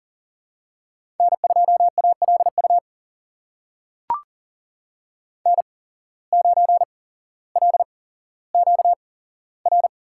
Morse code, heard as a keyed tone, about 40 wpm, spaced as for 17 wpm, 700 Hz. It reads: N2ALU N9LQR
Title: USA Call Signs - 5 Characters - 8x Character Spacing - 40wpm